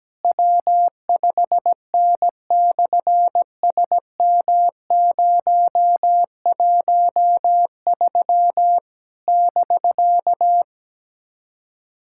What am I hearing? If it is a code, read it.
Morse code, heard as a keyed tone, about 17 wpm, 700 Hz. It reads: W5N/SM013 <BK>
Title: SOTA QSO Practice 17wpm